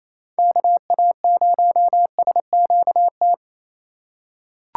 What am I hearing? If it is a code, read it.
Morse code, heard as a keyed tone, about 28 wpm, 700 Hz. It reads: KA0SQT
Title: USA Call Signs - 6 Characters - 2x Character Spacing - 28wpm